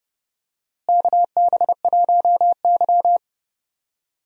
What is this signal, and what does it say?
Morse code, heard as a keyed tone, about 29 wpm, 700 Hz. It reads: KB1Y
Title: USA Call Signs - 4 Characters - Repeated 3x - 30wpm